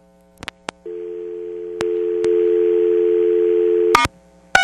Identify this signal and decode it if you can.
Dial tone (North America), 350 + 440 Hz.
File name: TurtleCreekEWSD